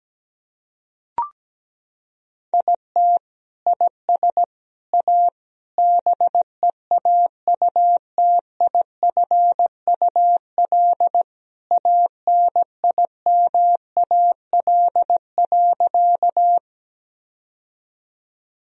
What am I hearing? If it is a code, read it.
Morse code, heard as a keyed tone, about 17 wpm, 700 Hz. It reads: IT IS A BEAUTIFUL ANIMAL.
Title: Sentences from Top 1000 Words - Review Part 1 - 17wpm